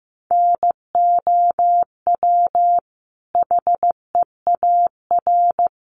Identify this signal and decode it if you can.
Morse code, heard as a keyed tone, about 14 wpm, 700 Hz. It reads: NOW HEAR
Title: Sentences from Top 1100 Words 15wpm